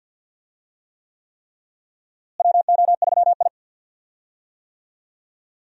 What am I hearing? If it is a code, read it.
Morse code, heard as a keyed tone, about 53 wpm, 700 Hz. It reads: WO3I
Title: USA Call Signs - 4 Characters - Repeated 3x - 50wpm